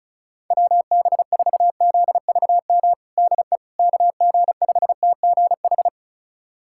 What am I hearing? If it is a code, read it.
Morse code, heard as a keyed tone, about 34 wpm, 700 Hz. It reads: WB4ZVM DE KG5TGH
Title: USA Callsign DE Callsign - 6 Characters - 35wpm